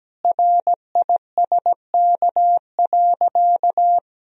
Morse code, heard as a keyed tone, about 17 wpm, 700 Hz. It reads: RISK.
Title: Sentences from Top 1000 Words - Review Part 4 - 17wpm